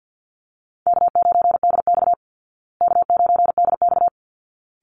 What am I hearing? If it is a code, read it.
Morse code, heard as a keyed tone, about 49 wpm, 700 Hz. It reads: K9DX K9DX
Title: Call Signs Repeated 3x - 50wpm